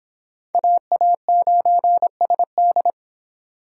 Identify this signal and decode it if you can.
Morse code, heard as a keyed tone, about 25 wpm, 700 Hz. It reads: AA9SD